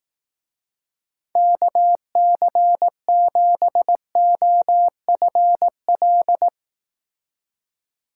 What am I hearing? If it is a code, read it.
Morse code, heard as a keyed tone, about 18 wpm, 700 Hz. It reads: KC7OFL
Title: Speed Racing - USA 2x3 Call Signs 15wpm